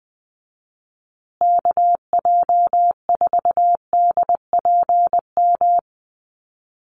Morse code, heard as a keyed tone, about 20 wpm, 700 Hz. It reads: KJ4DPM
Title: USA Call Signs Repeated 2x 20wpm